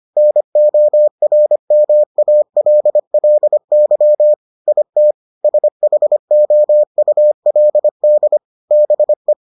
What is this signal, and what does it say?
Morse code, heard as a keyed tone, about 25 wpm, 600 Hz. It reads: NORMALLY IT SHOULD BE